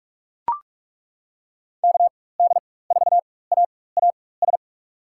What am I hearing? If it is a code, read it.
Morse code, heard as a keyed tone, about 42 wpm, spaced as for 20 wpm, 690 Hz. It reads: KD4AAS
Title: USA Call Signs - 4x Character Spacing - 45wpm